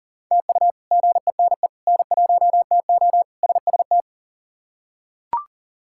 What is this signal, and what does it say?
Morse code, heard as a keyed tone, about 40 wpm, 700 Hz. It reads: TU GENE N1TO SST
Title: K1USN SST QSO Practice - 40wpm